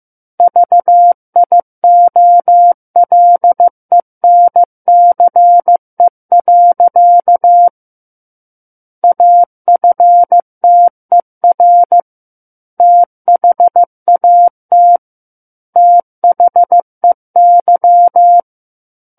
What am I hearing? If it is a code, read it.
Morse code, heard as a keyed tone, about 15 wpm, 700 Hz. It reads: VIOLENCE. AFTER THAT THEY
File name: Book One / 06-Chapter 6 - 15 WPM